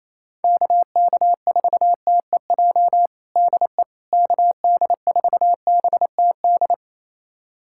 Morse code, heard as a keyed tone, about 28 wpm, 700 Hz. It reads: KK4TEJ DE KD4BTD
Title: USA Callsign DE Callsign - 6 Characters - 28wpm